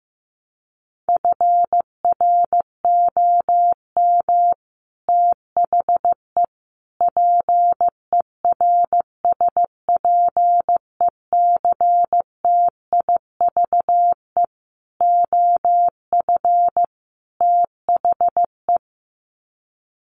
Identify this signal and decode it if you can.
Morse code, heard as a keyed tone, about 15 wpm, 700 Hz. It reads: FROM THE PERSPECTIVE OF THE
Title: Sets of 5 Words Encore2 15wpm